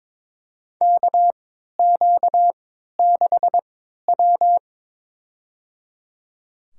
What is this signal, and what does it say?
Morse code, heard as a keyed tone, about 22 wpm, spaced as for 12 wpm, 700 Hz. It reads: KQ6W
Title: USA Call Signs - 3x Character Spacing - 22wpm